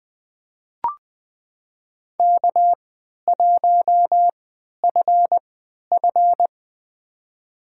Morse code, heard as a keyed tone, about 20 wpm, spaced as for 11 wpm, 700 Hz. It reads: K1FF
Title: USA Call Signs - 4 Characters - 3x Character Spacing - 20wpm